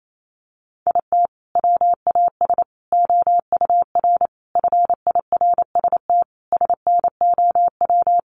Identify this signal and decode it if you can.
Morse code, heard as a keyed tone, about 28 wpm, 700 Hz. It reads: IT WAS OUR FIRST SNOW
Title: Sentences from Top 1500 Words 28wpm